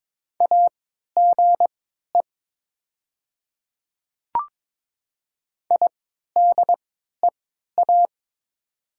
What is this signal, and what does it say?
Morse code, heard as a keyed tone, about 22 wpm, spaced as for 13 wpm, 700 Hz. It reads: AGE IDEA
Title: 3x Character Spacing - Top 500 Words - 22wpm